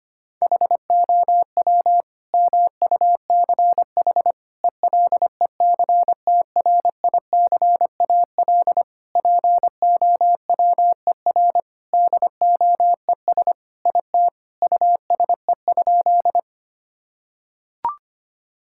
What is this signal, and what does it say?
Morse code, heard as a keyed tone, about 25 wpm, 700 Hz. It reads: HOW MUCH ELECTRICAL POWER DOES IT USE?
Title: Sentences from Top 1000 Words - Review Part 6 - 25wpm